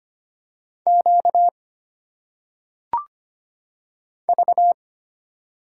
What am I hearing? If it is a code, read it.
Morse code, heard as a keyed tone, about 25 wpm, 700 Hz. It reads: Q V
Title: Single Letters - Phonetic Pronunciation 25wpm